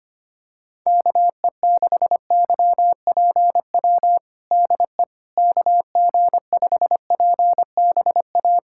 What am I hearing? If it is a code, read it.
Morse code, heard as a keyed tone, about 25 wpm, 700 Hz. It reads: KE6YPW DE KG5PBA
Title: USA Callsign DE Callsign - 6 Characters - 25wpm